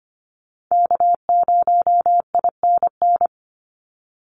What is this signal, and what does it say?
Morse code, heard as a keyed tone, about 25 wpm, 700 Hz. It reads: K0INN
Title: Call Signs Repeated 2x - 25wpm